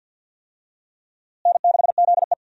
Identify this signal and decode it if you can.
Morse code, heard as a keyed tone, about 52 wpm, 700 Hz. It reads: N6ZE
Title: USA Call Signs - 4 Characters - Repeated 3x - 50wpm